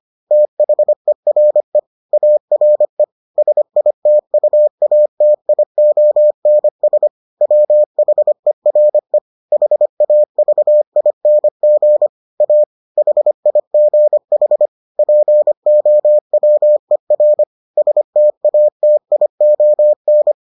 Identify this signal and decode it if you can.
Morse code, heard as a keyed tone, about 25 wpm, 600 Hz. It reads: THERE ARE SITUATIONS WHERE HAVING A HIGH POWER STATION